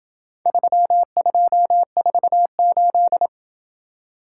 Morse code, heard as a keyed tone, about 27 wpm, 700 Hz. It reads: 3248